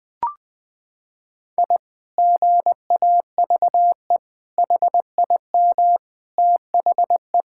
Morse code, heard as a keyed tone, about 21 wpm, 700 Hz. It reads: I GAVE HIM THE
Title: Sentences from Top 1000 Words - Review Part 6 - 20wpm